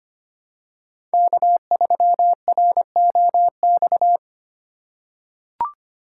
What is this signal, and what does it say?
Morse code, heard as a keyed tone, about 25 wpm, 700 Hz. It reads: K3ROX
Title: USA Call Signs - 5 Characters - 2x Character Spacing - 25wpm